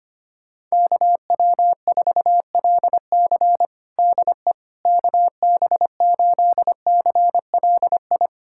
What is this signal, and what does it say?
Morse code, heard as a keyed tone, about 25 wpm, 700 Hz. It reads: KW4LC DE KB8CLI
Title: USA Callsign DE Callsign 25wpm